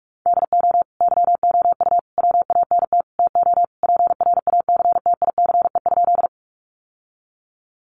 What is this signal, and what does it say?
Morse code, heard as a keyed tone, about 44 wpm, 700 Hz. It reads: DO YOU WANT TO PRACTICE?
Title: Sentences from Top 700 Words 45wpm